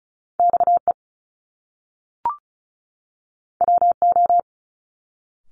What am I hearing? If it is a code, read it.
Morse code, heard as a keyed tone, about 36 wpm, 700 Hz. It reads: XE WO
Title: Top Two Letter Combinations in Words 35wpm